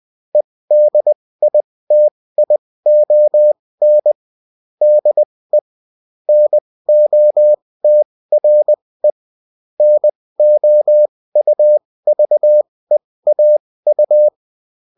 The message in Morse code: EDITION DE NOTRE NOUVEAU